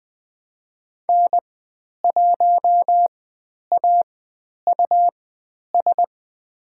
Morse code, heard as a keyed tone, about 20 wpm, spaced as for 10 wpm, 700 Hz. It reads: N1AUS
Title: 109 Lesson 18g - Letter m - Call Signs 20wpm